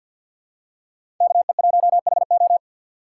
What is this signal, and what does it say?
Morse code, heard as a keyed tone, about 50 wpm, 700 Hz. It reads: KE1HO
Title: USA Call Signs - 5 Characters - 2x Character Spacing - 50wpm